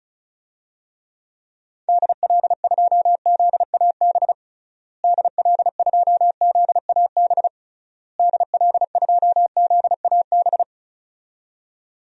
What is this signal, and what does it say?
Morse code, heard as a keyed tone, about 36 wpm, 700 Hz. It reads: DL2ZAB DL2ZAB DL2ZAB
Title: DXCC-100 Call Signs - 6 Characters - Repeated 3x - 35wpm